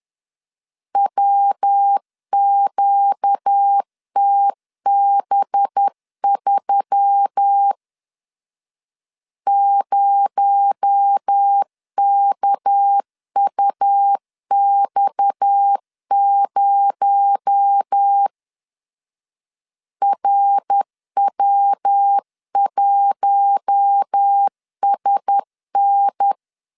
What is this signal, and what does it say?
Morse code, heard as a keyed tone, about 10 wpm, 800 Hz. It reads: WQTB3 0KUX0 RW1SN
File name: LEZIONI / 6 PKX0 / 6 PKX0 10wpm 2707